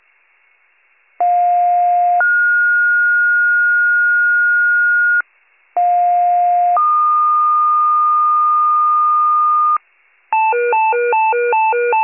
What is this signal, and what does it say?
Two-tone page, heard as a one-second tone, then a three-second tone, about 710 then 1430 Hz, about 710 then 1220 Hz, with noise. It is heard from 1.2 to 9.8 s.